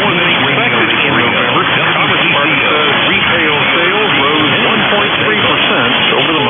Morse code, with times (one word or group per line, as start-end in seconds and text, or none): none